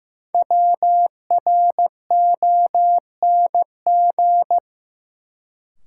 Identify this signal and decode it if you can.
Morse code, heard as a keyed tone, about 15 wpm, 700 Hz. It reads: WRONG
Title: Sets of 5 Words Encore - New Word Drill - 15wpm